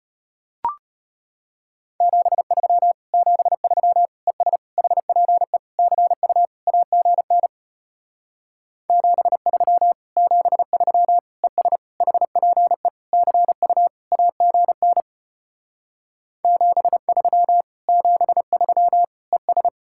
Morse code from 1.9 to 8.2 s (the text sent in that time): 73 73 ES HPE CU AGN